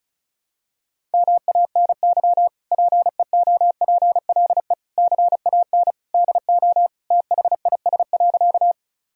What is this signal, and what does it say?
Morse code, heard as a keyed tone, about 36 wpm, 700 Hz. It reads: MANY PEOPLE CAN DO THIS.
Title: Sentences from Top 1000 Words - Review Part 1 - 35wpm